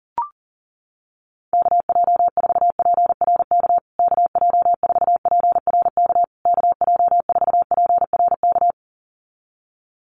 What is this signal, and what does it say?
Morse code, heard as a keyed tone, about 40 wpm, 700 Hz. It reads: KJ4PRK KJ4PRK KJ4PRK
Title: USA Call Signs Repeated 3x 40wpm